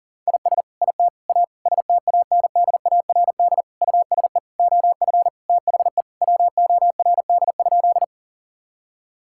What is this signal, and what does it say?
Morse code, heard as a keyed tone, about 40 wpm, 700 Hz. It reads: IS IT A STANDARD USE OF THE WORD?
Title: Sentences from Top 1000 Words - Review Part 6 - 40wpm